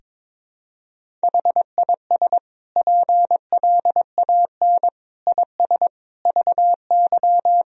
Morse code, heard as a keyed tone, about 22 wpm, 700 Hz. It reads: HIS PLAN IS VY
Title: Sentences from Top 500 Words and Abbreviations 22wpm